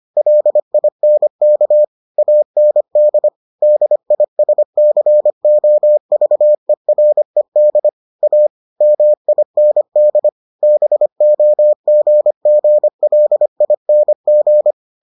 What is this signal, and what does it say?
Morse code, heard as a keyed tone, about 25 wpm, 600 Hz. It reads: LINK AND DISCOVERED A MIND BOGGLING